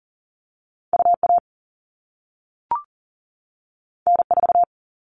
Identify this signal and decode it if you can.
Morse code, heard as a keyed tone, about 40 wpm, 700 Hz. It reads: UA N4
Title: All Cut Numbers - 2-Digit 40wpm